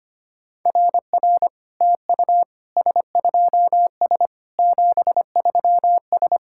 Morse code, heard as a keyed tone, about 25 wpm, 700 Hz. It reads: RR TU S2S 73S
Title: SOTA QSO Practice 25wpm